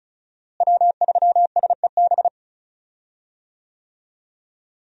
Morse code, heard as a keyed tone, about 36 wpm, 700 Hz. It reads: W3SEB
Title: USA Call Signs - 5 Characters - Repeated 2x - 35wpm